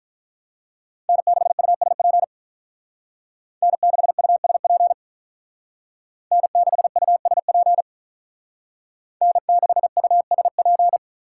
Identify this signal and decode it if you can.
Morse code, heard as a keyed tone, about 43 wpm, 700 Hz. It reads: N6USP N6USP N6USP N6USP
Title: Speed Racing - USA 1x3 Call Signs 35wpm